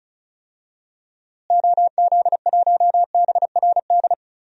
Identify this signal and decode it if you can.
Morse code, heard as a keyed tone, about 36 wpm, 700 Hz. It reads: OZ1BRD
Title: DXCC-100 Call Signs - 6 Characters - Repeated 3x - 35wpm